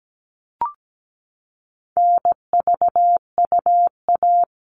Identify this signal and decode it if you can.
Morse code, heard as a keyed tone, about 17 wpm, 700 Hz. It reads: NVUA
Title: All Cut Numbers - 4-Digit 17wpm